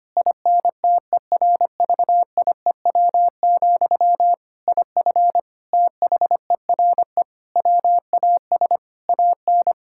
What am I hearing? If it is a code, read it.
Morse code, heard as a keyed tone, about 25 wpm, 700 Hz. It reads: INTERVIEW, IF THERE WAS AN
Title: Sentences from Top 1600 Words 25wpm